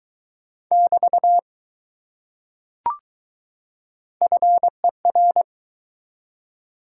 Morse code, heard as a keyed tone, about 20 wpm, 700 Hz. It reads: <BT> FER